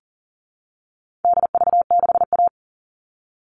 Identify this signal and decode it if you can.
Morse code, heard as a keyed tone, about 40 wpm, 700 Hz. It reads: DV6A
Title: All Cut Numbers - 4-Digit 40wpm